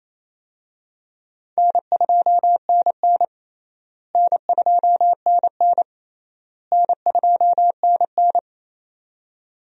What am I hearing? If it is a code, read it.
Morse code, heard as a keyed tone, about 28 wpm, 700 Hz. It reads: N2NN N2NN N2NN
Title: USA Call Signs - 4 Characters - Repeated 3x - 28wpm